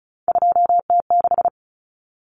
Morse code, heard as a keyed tone, about 34 wpm, 700 Hz. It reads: 2T6
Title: Common Cut Numbers - 3-Digit 35wpm